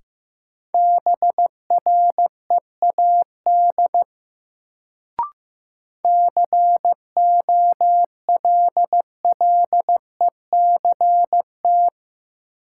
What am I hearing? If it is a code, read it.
Morse code, heard as a keyed tone, about 15 wpm, 700 Hz. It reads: BREAD COLLECT
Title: Sentences from Top 1600 Words - New Word Drill - 15wpm